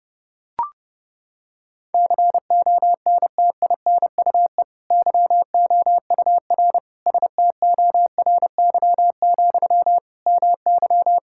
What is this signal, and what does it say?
Morse code, heard as a keyed tone, about 30 wpm, 700 Hz. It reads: CONTINUE YOUR STORY, MY
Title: Sentences from Top 1000 Words - Review Part 6 - 30wpm